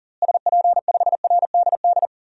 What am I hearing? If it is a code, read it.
Morse code, heard as a keyed tone, about 40 wpm, 700 Hz. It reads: SP5RDD
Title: DXCC-100 Call Signs - 6 Characters - Repeated 2x - 40wpm